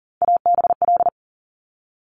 Morse code, heard as a keyed tone, about 38 wpm, 700 Hz. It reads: ABL
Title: Top Three Letter Combinations in Words 40wpm